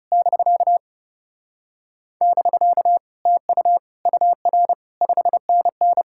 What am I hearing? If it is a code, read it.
Morse code, heard as a keyed tone, about 33 wpm, 700 Hz. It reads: <BK> <BK> TU UR 5NN
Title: Speed-Racing Common QSO Phrases - 30wpm